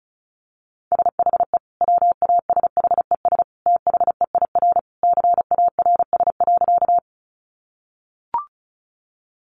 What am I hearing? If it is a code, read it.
Morse code, heard as a keyed tone, about 34 wpm, 700 Hz. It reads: SHE WASHES THEIR CARS.
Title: Easy Phrases 35wpm